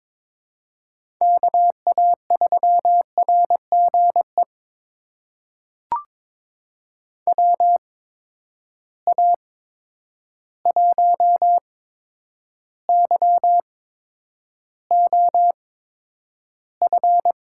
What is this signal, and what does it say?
Morse code, heard as a keyed tone, about 22 wpm, spaced as for 9 wpm, 700 Hz. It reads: KA3RGE WA1YOF
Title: USA Call Signs - 6 Characters - 8x Character Spacing - 22wpm